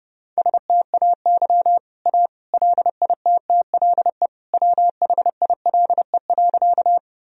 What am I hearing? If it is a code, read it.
Morse code, heard as a keyed tone, about 30 wpm, 700 Hz. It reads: STAY A LITTLE WHILE.
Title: Sentences from Top 1000 Words - Review Part 1 - 30wpm